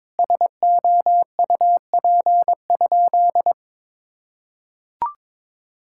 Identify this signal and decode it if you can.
Morse code, heard as a keyed tone, about 22 wpm, 700 Hz. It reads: SOUP?
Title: Sentences from Top 1000 Words - Review Part 6 - 22wpm